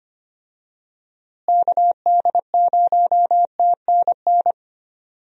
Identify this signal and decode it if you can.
Morse code, heard as a keyed tone, about 25 wpm, 700 Hz. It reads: KD0TNN
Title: USA Call Signs - 6 Characters - Repeated 3x - 25wpm